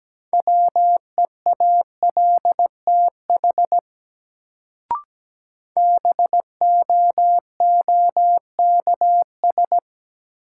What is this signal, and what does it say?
Morse code, heard as a keyed tone, about 17 wpm, 700 Hz. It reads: WEALTH BOOKS
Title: Sentences from Top 1000 Words - Review Part 4 - New Word Drill - 17wpm